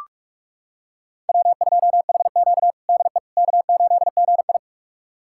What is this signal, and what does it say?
Morse code, heard as a keyed tone, about 44 wpm, 700 Hz. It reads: W2HQ DE K8GI
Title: USA Callsign DE Callsign - 4 Characters - 45wpm